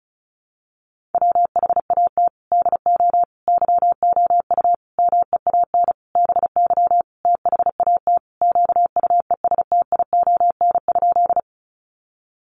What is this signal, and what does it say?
Morse code, heard as a keyed tone, about 34 wpm, 700 Hz. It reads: WHAT DO YOU MEAN BY THAT QUESTION?
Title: Sentences from Top 400 Words 35wpm